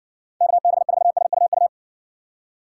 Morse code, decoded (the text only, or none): KB4SUU